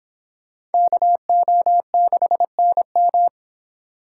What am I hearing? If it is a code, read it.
Morse code, heard as a keyed tone, about 25 wpm, 700 Hz. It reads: KO6NM